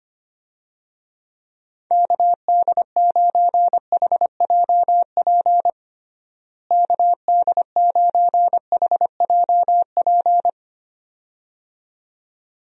Morse code, heard as a keyed tone, about 25 wpm, 700 Hz. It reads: KD9HJP KD9HJP
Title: USA Call Signs - 6 Characters - Repeated 2x - 25wpm